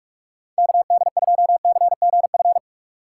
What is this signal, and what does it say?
Morse code, heard as a keyed tone, about 56 wpm, 700 Hz. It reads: KD2CGF
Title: Speed Racing - USA 2x3 Call Signs 45wpm